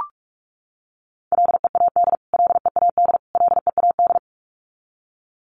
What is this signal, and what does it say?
Morse code, heard as a keyed tone, about 44 wpm, 700 Hz. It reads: LEAD LEAD LEAD
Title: Top 500 Words Repeated 45wpm